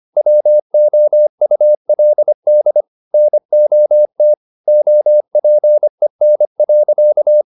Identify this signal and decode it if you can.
Morse code, heard as a keyed tone, about 25 wpm, 600 Hz. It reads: WOULD NOT OPEN.
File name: the old man and the sea part 1 25wpm